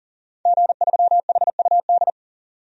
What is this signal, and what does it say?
Morse code, heard as a keyed tone, about 40 wpm, 700 Hz. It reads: G3HUD
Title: DXCC-100 Call Signs - 5 Characters - 40wpm